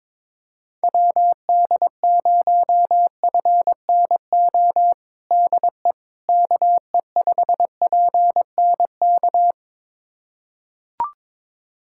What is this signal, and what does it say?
Morse code, heard as a keyed tone, about 22 wpm, 700 Hz. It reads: WD0FNO DE KE5PNK
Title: USA Callsign DE Callsign - 6 Characters - 22wpm